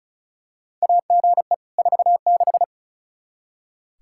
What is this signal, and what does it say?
Morse code, heard as a keyed tone, about 34 wpm, 700 Hz. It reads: AGE 46
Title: Practice QSO - Exchange 3 - More Info - 35wpm